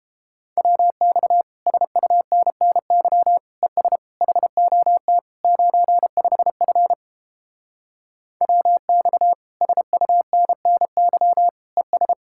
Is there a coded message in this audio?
Yes: Morse code from 0.6 to 12.1 s, heard as a keyed tone, about 29 wpm, 700 Hz. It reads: WX SUNNY ES HOT 95F WX SUNNY ES